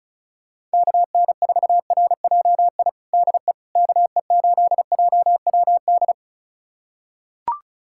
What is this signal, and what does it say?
Morse code, heard as a keyed tone, about 36 wpm, 700 Hz. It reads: KN4RJI DE KE8JWD
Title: USA Callsign DE Callsign - 6 Characters - 35wpm